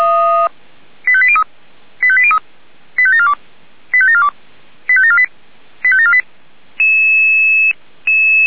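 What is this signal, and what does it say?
Five-tone selective call (ZVEI), heard as quick runs of five tones, tones of about 70 ms. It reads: 85803 85803 85742 85742 85748 85748
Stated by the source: Five-tone ZVEI selective call received by SDR.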